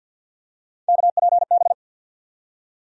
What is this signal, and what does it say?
Morse code, heard as a keyed tone, about 47 wpm, 700 Hz. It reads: KPB
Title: USA Call Sign Postfixes - 3 Letters 50wpm